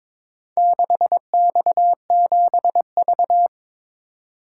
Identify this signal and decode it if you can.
Morse code, heard as a keyed tone, about 23 wpm, 700 Hz. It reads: 6X7V
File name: practice / 22wpm / Four Letter-Number Combinations - 22wpm